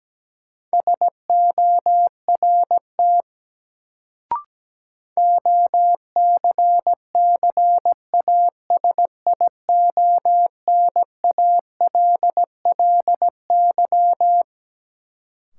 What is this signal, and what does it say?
Morse code, heard as a keyed tone, about 17 wpm, 700 Hz. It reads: SORT OCCASIONALLY
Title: Sentences from Top 1300 Words - New Word Drill Alt - 17wpm